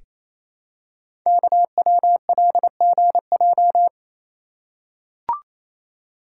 Morse code, heard as a keyed tone, about 29 wpm, 700 Hz. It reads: KWLGJ
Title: Code Groups - 5 Letters - 28wpm